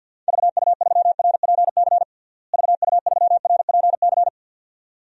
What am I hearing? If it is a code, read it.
Morse code, heard as a keyed tone, about 46 wpm, 700 Hz. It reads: VU3RPC VU3RPC
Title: DXCC-100 Call Signs - 6 Characters - Repeated 2x - 50wpm